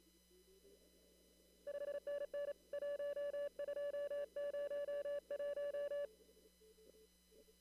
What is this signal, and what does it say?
Morse code, heard as a keyed tone, about 30 wpm, 580 Hz. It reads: 5NN 1201
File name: CW / kt5j wpx cw-op-k5pi